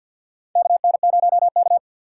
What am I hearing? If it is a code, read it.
Morse code, heard as a keyed tone, about 51 wpm, 700 Hz. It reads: KN0K
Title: USA Call Signs - 4 Characters - 6x Character Spacing - 50wpm